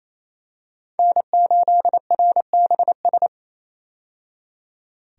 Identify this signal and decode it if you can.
Morse code, heard as a keyed tone, about 28 wpm, 700 Hz. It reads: N8RBS
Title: USA Call Signs - 5 Characters - 28wpm